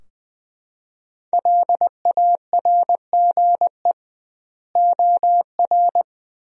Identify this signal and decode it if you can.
Morse code, heard as a keyed tone, about 20 wpm, 700 Hz. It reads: LARGE OR
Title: 2x Word-Spacing - Sentences from Top 200 Words - 20wpm